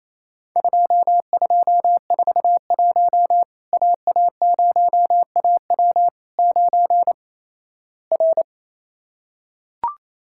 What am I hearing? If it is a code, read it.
Morse code, heard as a keyed tone, about 27 wpm, 700 Hz. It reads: 2241 AA0AW 9 R
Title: Worked All Europe DX Contest - QTC Exchange - 28wpm